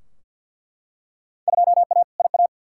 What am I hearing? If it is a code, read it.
Morse code, heard as a keyed tone, about 49 wpm, 700 Hz. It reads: 2A IA
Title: Field Day Exchanges Encore - 50wpm